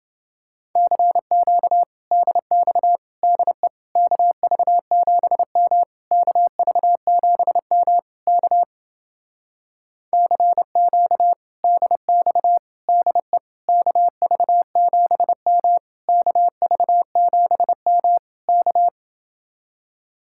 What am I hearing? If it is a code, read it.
Morse code, heard as a keyed tone, about 28 wpm, 700 Hz. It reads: CQ DX DE KV7M KV7M K CQ DX DE KV7M KV7M K